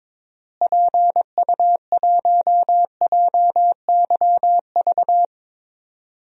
Morse code, heard as a keyed tone, about 22 wpm, 700 Hz. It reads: PU1JYV
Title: DXCC-100 Call Signs - 6 Characters - Repeated 2x - 22wpm